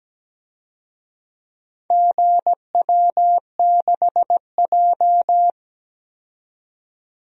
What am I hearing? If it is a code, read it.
Morse code, heard as a keyed tone, about 17 wpm, 700 Hz. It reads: GW6J
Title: DXCC-100 Call Signs - 4 Characters - 17wpm